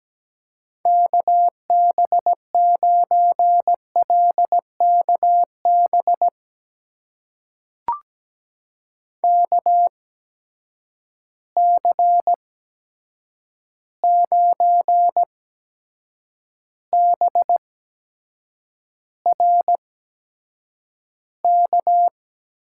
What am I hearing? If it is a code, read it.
Morse code, heard as a keyed tone, about 17 wpm, spaced as for 7 wpm, 700 Hz. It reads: KB9LKB KC9BRK
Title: USA Call Signs - 6 Characters - 8x Character Spacing - 17wpm